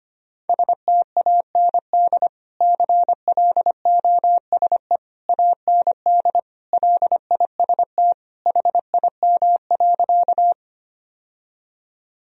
Morse code, heard as a keyed tone, about 25 wpm, 700 Hz. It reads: STAND CLOSE AND LIST HIM.
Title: Sentences from Top 1000 Words - Review Part 2 - 25wpm